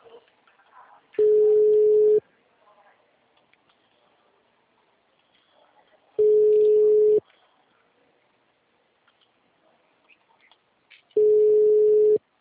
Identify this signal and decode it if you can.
Ringback tone (Europe, 425 Hz), with speech.